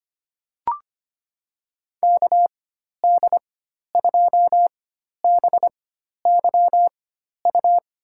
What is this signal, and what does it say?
Morse code, heard as a keyed tone, about 25 wpm, spaced as for 12 wpm, 700 Hz. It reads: KD2BYU
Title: USA Call Signs - 6 Characters - 4x Character Spacing - 25wpm